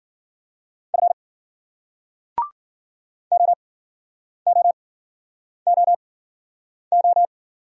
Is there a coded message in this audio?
Yes: Morse code from 0.9 to 7.3 s, heard as a keyed tone, about 54 wpm, spaced as for 9 wpm, 700 Hz. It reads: F OOOO